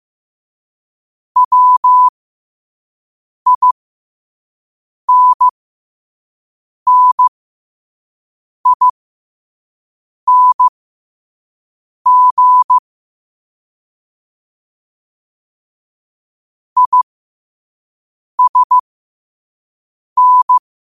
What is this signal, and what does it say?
Morse code, heard as a keyed tone, about 15 wpm, spaced as for 5 wpm, 1000 Hz. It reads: WINNING ISN
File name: qotdfiles / 5wpm